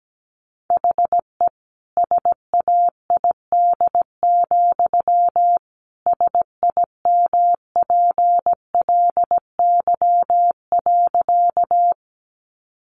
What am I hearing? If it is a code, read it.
Morse code, heard as a keyed tone, about 17 wpm, 700 Hz. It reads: HE SAID, SIMPLY.
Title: Sentences from Top 600 Words 17wpm